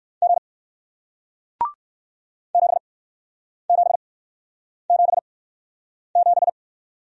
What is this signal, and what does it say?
Morse code, heard as a keyed tone, about 61 wpm, spaced as for 9 wpm, 700 Hz. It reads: G 7777